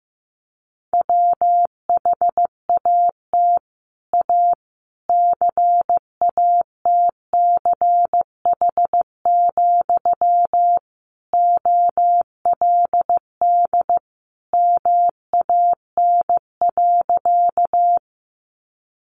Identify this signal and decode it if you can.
Morse code, heard as a keyed tone, about 14 wpm, 700 Hz. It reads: WHAT A CATCH, OLD MAN.
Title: Sentences from Top 1500 Words 15wpm